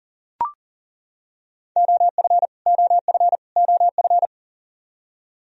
Morse code, heard as a keyed tone, about 40 wpm, 700 Hz. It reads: OF OF OF
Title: Top 100 Words Repeated 40wpm